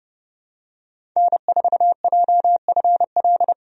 Morse code, heard as a keyed tone, about 30 wpm, 700 Hz. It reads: N4JFL
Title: USA Call Signs - 5 Characters - 30wpm